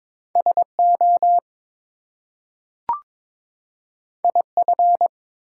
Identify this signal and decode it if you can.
Morse code, heard as a keyed tone, about 22 wpm, 700 Hz. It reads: SO IF
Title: Two Letter Words - 22wpm